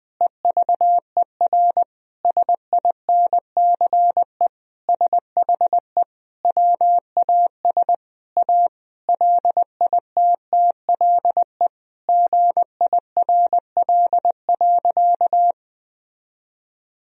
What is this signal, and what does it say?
Morse code, heard as a keyed tone, about 20 wpm, 700 Hz. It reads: EVER SINCE SHE WAS A LITTLE GIRL.
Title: Sentences from Top 1000 Words - Review Part 1 - 20wpm